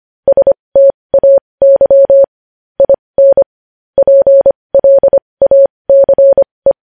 Morse code, heard as a keyed tone, about 25 wpm, spaced as for 20 wpm, 550 Hz. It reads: STAY IN PLACE
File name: podcast / News headlines-1x-25x20